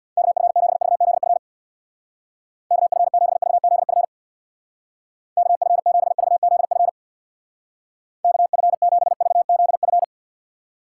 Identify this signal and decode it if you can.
Morse code, heard as a keyed tone, about 64 wpm, 700 Hz. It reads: KF7VZF KF7VZF KF7VZF KF7VZF